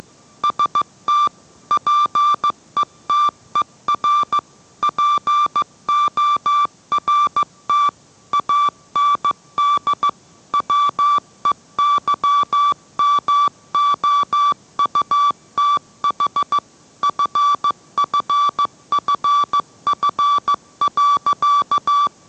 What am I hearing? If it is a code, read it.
Morse code, heard as a keyed tone, about 15 wpm, 1160 Hz. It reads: ST PETER PORT AND WEYMOUTH FFFF.